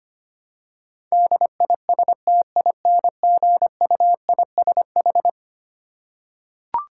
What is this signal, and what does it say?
Morse code, heard as a keyed tone, about 25 wpm, 700 Hz. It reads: DISTINGUISH
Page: morse-code-ninja-practice-25wpm